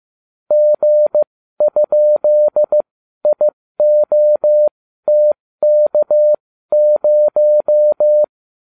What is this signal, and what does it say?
Morse code, heard as a keyed tone, about 15 wpm, spaced as for 12 wpm, 600 Hz. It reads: G?IOTK0
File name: cw / ubacw31 1